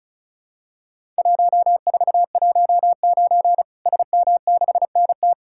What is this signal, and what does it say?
Morse code, heard as a keyed tone, about 36 wpm, 700 Hz. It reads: 1419 SM6NT
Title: Worked All Europe DX Contest - QTC Exchange - 35wpm